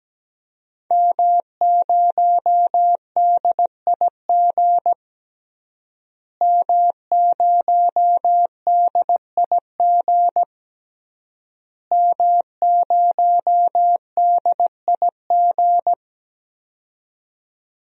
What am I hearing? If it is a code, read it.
Morse code, heard as a keyed tone, about 17 wpm, 700 Hz. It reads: M0DIG M0DIG M0DIG
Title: DXCC-100 Call Signs - 5 Characters - Repeated 3x - 17wpm